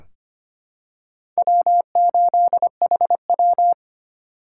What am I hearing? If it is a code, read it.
Morse code, heard as a keyed tone, about 25 wpm, 700 Hz. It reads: W8HW
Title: K1USN SST QSO Practice - 25wpm